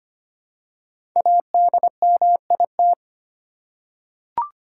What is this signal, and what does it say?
Morse code, heard as a keyed tone, about 25 wpm, 700 Hz. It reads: ADMIT
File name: 5 Letter Words - First 3 Letters Given - 25wpm